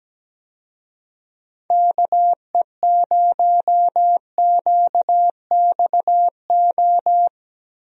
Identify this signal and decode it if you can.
Morse code, heard as a keyed tone, about 17 wpm, 700 Hz. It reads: KE0QXO
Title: USA Call Signs - 6 Characters - Repeated 3x - 17wpm